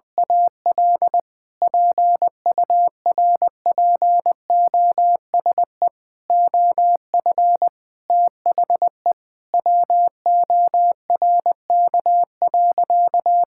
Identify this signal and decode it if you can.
Morse code, heard as a keyed tone, about 20 wpm, 700 Hz. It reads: AL PURPOSE OF THE WORK.
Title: Sentences from Top 1000 Words - Review Part 6 - 20wpm